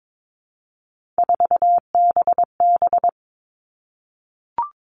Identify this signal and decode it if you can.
Morse code, heard as a keyed tone, about 22 wpm, 700 Hz. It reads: 4BB
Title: All Cut Numbers - 3-Digit 22wpm